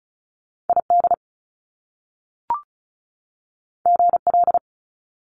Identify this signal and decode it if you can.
Morse code, heard as a keyed tone, about 36 wpm, 700 Hz. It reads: ID GL
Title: Top Two Letter Combinations in Words 35wpm